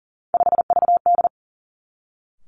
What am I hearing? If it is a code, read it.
Morse code, heard as a keyed tone, about 42 wpm, 700 Hz. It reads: 5VD
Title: Three Challenging Letter-Number Combinations 40wpm